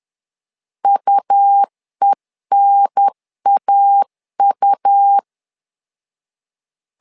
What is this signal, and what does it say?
Morse code, heard as a keyed tone, about 10 wpm, 800 Hz. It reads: UENAU
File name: LEZIONI / 2 UHDN / 2 UHDN 10wpm 1904